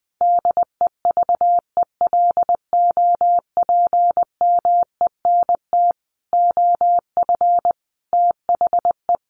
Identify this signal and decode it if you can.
Morse code, heard as a keyed tone, about 20 wpm, 700 Hz. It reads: DEVELOPMENT OF THE
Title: Sets of 5 Words Encore2 20wpm